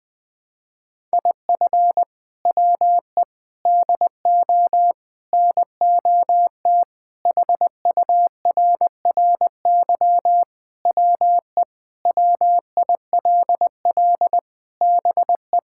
Morse code, heard as a keyed tone, about 20 wpm, 700 Hz. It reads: IF WE DO NOT HURRY WE WILL BE
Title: Sentences with 4-Word Phrases - Spoken Beginning - 20wpm